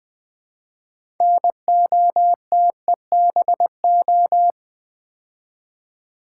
Morse code, heard as a keyed tone, about 20 wpm, 700 Hz. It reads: NOTEBO
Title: Build A Word - 2 Letters At A Time - 8-Letter Words - 20wpm